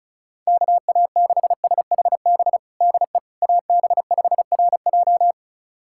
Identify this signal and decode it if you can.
Morse code, heard as a keyed tone, about 36 wpm, 700 Hz. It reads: KA6SHB DE AB5RJ
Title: USA Callsign DE Callsign 35wpm